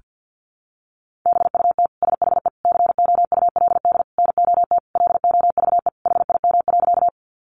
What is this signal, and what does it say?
Morse code, heard as a keyed tone, about 50 wpm, 700 Hz. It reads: BUT SHE COULD NOT LOVE HIM.
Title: Sentences from Top 600 Words 50wpm